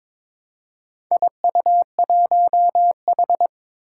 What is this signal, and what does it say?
Morse code, heard as a keyed tone, about 22 wpm, 700 Hz. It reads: IU1H
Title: Four Challenging Letter-Number Combinations 22wpm